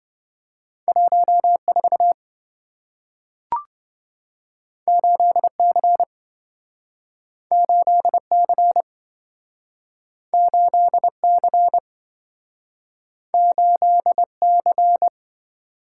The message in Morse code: 14 8C 8C 8C 8C

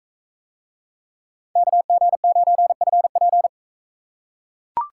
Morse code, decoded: KG9FP

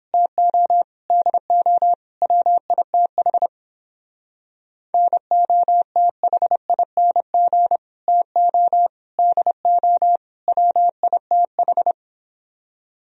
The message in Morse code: TO DO WITH NOTHING TO DO WITH

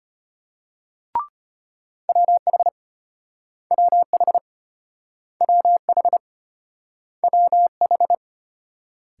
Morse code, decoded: WH WH WH WH

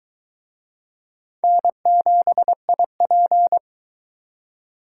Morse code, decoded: N7IP

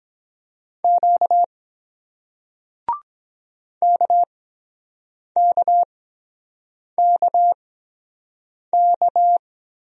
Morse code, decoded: Q KKKK

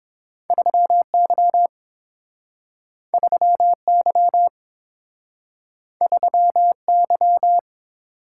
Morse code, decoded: 3Y 3Y 3Y

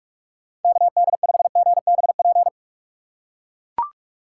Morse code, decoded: KD5GBP

